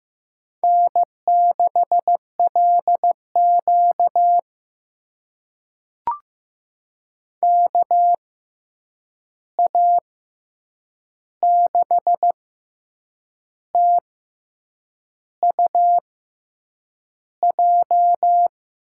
Morse code, heard as a keyed tone, about 15 wpm, spaced as for 7 wpm, 700 Hz. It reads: N6LQ KA6TUJ
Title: USA Call Signs - 6x Character Spacing - 15wpm